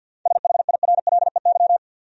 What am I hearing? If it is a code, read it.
Morse code, heard as a keyed tone, about 47 wpm, 700 Hz. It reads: SHIRLEY
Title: USA Common Names - 50wpm